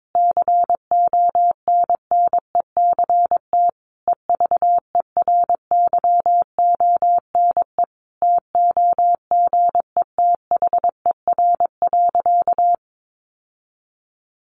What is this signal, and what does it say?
Morse code, heard as a keyed tone, about 22 wpm, 700 Hz. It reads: CONNECT EVERYONE TOGETHER.
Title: Sentences from Top 1500 Words 22wpm